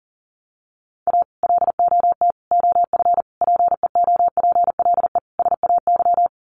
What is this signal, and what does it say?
Morse code, heard as a keyed tone, about 40 wpm, 700 Hz. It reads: A LOT OF PEOPLE SAY
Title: Sets of 5 Words Encore2 40wpm